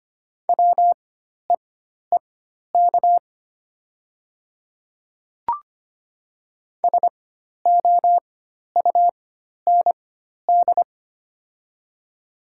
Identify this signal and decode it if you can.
Morse code, heard as a keyed tone, about 25 wpm, spaced as for 12 wpm, 700 Hz. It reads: WEEK SOUND
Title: 4x Character Spacing - Top 500 Words - 25wpm